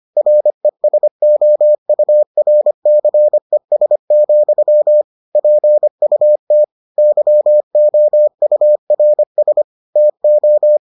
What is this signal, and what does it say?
Morse code, heard as a keyed tone, about 25 wpm, 600 Hz. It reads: RESOURCES, PUT YOURS TO